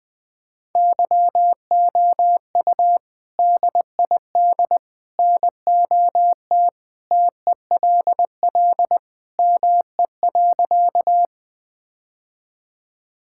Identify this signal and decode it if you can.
Morse code, heard as a keyed tone, about 20 wpm, 700 Hz. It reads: YOU DID NOT TELL ME.
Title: Sentences with 3-Word Phrases - Spoken Beginning - 20wpm